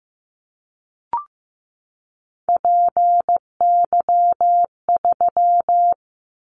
Morse code, heard as a keyed tone, about 15 wpm, 700 Hz. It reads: PY3
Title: Callsign Prefixes for Top 100 DXCC - 15wpm